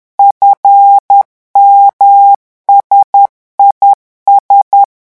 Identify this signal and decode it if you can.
Morse code, heard as a keyed tone, about 10 wpm, 800 Hz. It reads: FMSIS
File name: morse mp3 / revision11 lettres vitesse 10